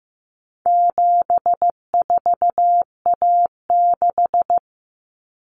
Morse code, heard as a keyed tone, about 15 wpm, 700 Hz. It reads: 74A6
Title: Common Cut Numbers - 4-Digit 15wpm